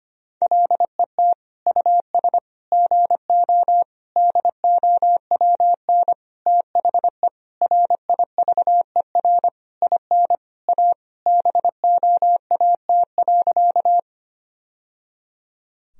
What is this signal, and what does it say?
Morse code, heard as a keyed tone, about 25 wpm, 700 Hz. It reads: LET US GO DOWN THE RIVER IN A BOAT.
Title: Sentences from Top 1000 Words - Review Part 3 - 25wpm